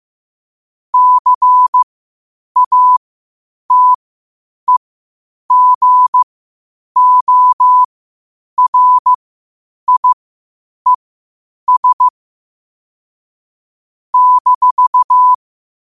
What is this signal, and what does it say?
Morse code, heard as a keyed tone, about 15 wpm, spaced as for 8 wpm, 1000 Hz. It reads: CATEGORIES -
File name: qotdfiles / 8wpm